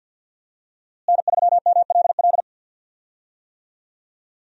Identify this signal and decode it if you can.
Morse code, heard as a keyed tone, about 45 wpm, 700 Hz. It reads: N3MLL
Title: USA Call Signs - 5 Characters - Repeated 3x - 50wpm